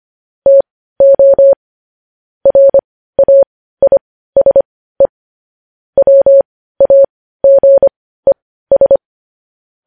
Morse code, heard as a keyed tone, about 25 wpm, spaced as for 15 wpm, 550 Hz. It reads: TO RAISE WAGES
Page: news headlines in morse code at 15 wpm